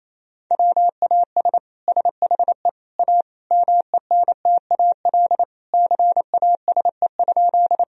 Morse code, heard as a keyed tone, about 28 wpm, 700 Hz. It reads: WAS SHE A MENTAL CASE?
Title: Sentences from Top 1000 Words - Review Part 4 - 28wpm